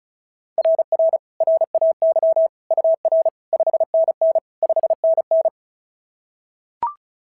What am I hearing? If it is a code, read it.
Morse code, heard as a keyed tone, about 36 wpm, 690 Hz. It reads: RR RAY UR 5NN 5NN
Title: SOTA QSO Practice 35wpm